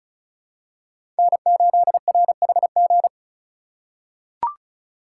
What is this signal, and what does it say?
Morse code, heard as a keyed tone, about 36 wpm, 700 Hz. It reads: N8RHG
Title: USA Call Signs - 5 Characters - 3x Character Spacing - 35wpm